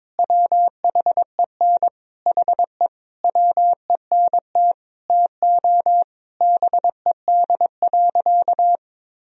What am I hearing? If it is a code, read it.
Morse code, heard as a keyed tone, about 22 wpm, 700 Hz. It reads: WHEN HE WENT TO BED.
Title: Sentences from Top 1000 Words - Review Part 3 - 22wpm